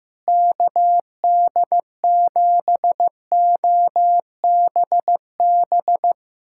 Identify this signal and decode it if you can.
Morse code, heard as a keyed tone, about 15 wpm, 700 Hz. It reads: KD7OBB
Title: USA Callsign DE Callsign - 6 Characters - 15wpm